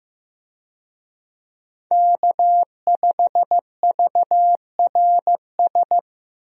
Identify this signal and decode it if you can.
Morse code, heard as a keyed tone, about 15 wpm, 700 Hz. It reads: K5VRS
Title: USA Call Signs - 5 Characters - 15wpm